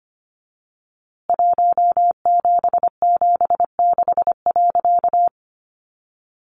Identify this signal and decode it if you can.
Morse code, heard as a keyed tone, about 24 wpm, 700 Hz. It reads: 1776.
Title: A Sound Of Thunder 25wpm